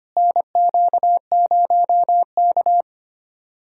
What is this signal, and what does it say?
Morse code, heard as a keyed tone, about 25 wpm, 700 Hz. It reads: NQ0K
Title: USA Callsign DE Callsign - 4 Characters - 25wpm